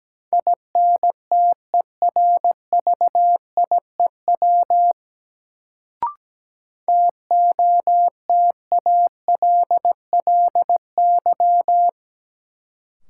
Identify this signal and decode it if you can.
Morse code, heard as a keyed tone, about 17 wpm, 700 Hz. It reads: INTERVIEW TOTALLY
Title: Sentences from Top 1600 Words - New Word Drill - 17wpm